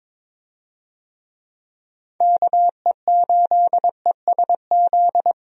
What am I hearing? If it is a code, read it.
Morse code, heard as a keyed tone, about 22 wpm, 700 Hz. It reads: KE8ESZ
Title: USA Call Signs - 6 Characters - 22wpm